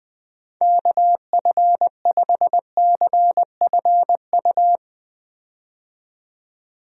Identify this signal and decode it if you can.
Morse code, heard as a keyed tone, about 20 wpm, 700 Hz. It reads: KF5CFU
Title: USA Call Signs - 6 Characters - 20wpm